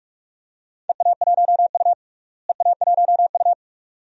Morse code, heard as a keyed tone, about 46 wpm, 700 Hz. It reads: EA1U EA1U
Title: DXCC-100 Call Signs - 4 Characters - Repeated 2x - 45wpm